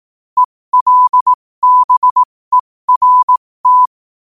Morse code, heard as a keyed tone, about 18 wpm, spaced as for 15 wpm, 1000 Hz. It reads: ELBERT
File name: qotdfiles / 15wpm